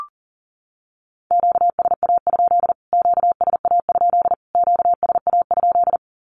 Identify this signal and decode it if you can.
Morse code, heard as a keyed tone, about 40 wpm, 700 Hz. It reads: QSA? QSA? QSA?
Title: Q-Codes Repeated 3x 40wpm